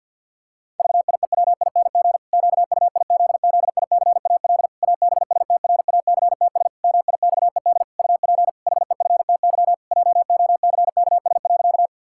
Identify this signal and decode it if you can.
Morse code, heard as a keyed tone, about 50 wpm, 700 Hz. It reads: VIEWING QUIZZICAL ABSTRACTS MIXED UP HEFTY JOCKS.
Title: Pangram Challenge v2 - 50wpm